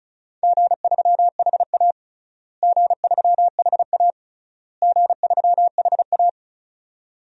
Morse code, heard as a keyed tone, about 36 wpm, 700 Hz. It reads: G3HA G3HA G3HA
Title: DXCC-100 Call Signs - 4 Characters - Repeated 3x - 35wpm